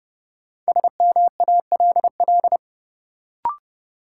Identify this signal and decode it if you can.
Morse code, heard as a keyed tone, about 30 wpm, 700 Hz. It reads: SMALL
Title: Sentences from Top 200 Words - New Word Drill - 30wpm